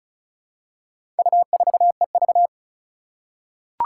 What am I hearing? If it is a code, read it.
Morse code, heard as a keyed tone, about 36 wpm, 700 Hz. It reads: U4EV
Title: Four Challenging Letter-Number Combinations 35wpm